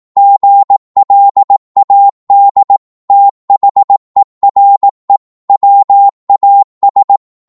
Morse code, heard as a keyed tone, about 18 wpm, 800 Hz. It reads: GLAD THERE WAS